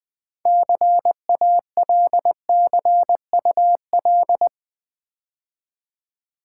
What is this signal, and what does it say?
Morse code, heard as a keyed tone, about 20 wpm, 700 Hz. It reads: CALCUL
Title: Build A Word - 3 Letters At A Time - 9-Letter Words - 20wpm